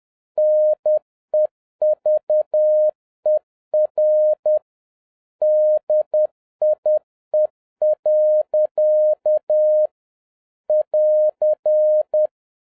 Morse code, heard as a keyed tone, about 10 wpm, 610 Hz. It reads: NEVER DIE. <AR>